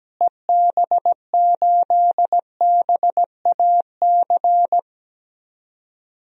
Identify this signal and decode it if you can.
Morse code, heard as a keyed tone, about 17 wpm, 700 Hz. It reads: EB8BAC
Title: DXCC-100 Call Signs - 6 Characters - Repeated 2x - 17wpm